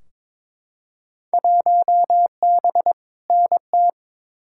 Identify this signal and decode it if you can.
Morse code, heard as a keyed tone, about 22 wpm, 700 Hz. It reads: 1B NT
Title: Field Day Exchanges Encore - 22wpm